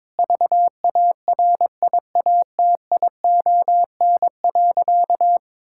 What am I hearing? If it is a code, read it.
Morse code, heard as a keyed tone, about 22 wpm, 700 Hz. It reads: VARIATION.
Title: Sentences from Top 1600 Words 22wpm